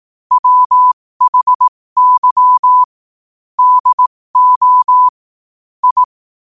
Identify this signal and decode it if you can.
Morse code, heard as a keyed tone, about 18 wpm, spaced as for 15 wpm, 1000 Hz. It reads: WHY DO I
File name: qotdfiles / 15wpm